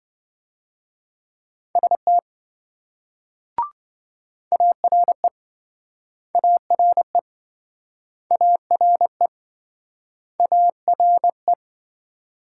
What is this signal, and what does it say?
Morse code, heard as a keyed tone, about 28 wpm, 700 Hz. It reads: ST ARE ARE ARE ARE